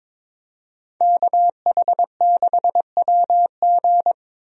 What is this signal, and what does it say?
Morse code, heard as a keyed tone, about 21 wpm, 700 Hz. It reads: KH6WG
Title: DXCC-100 Call Signs - 5 Characters - 22wpm